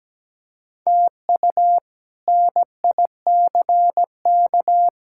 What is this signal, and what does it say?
Morse code, heard as a keyed tone, about 17 wpm, 700 Hz. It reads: TU NICK
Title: K1USN SST QSO Practice - 17wpm